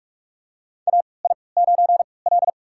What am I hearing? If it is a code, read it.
Morse code, heard as a keyed tone, about 43 wpm, spaced as for 25 wpm, 700 Hz. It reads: AI9L
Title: USA Call Signs - 4 Characters - 3x Character Spacing - 45wpm